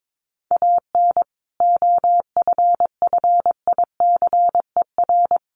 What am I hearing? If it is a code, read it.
Morse code, heard as a keyed tone, about 22 wpm, 700 Hz. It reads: AN OFFICER
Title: Sentences from Top 1200 Words 22wpm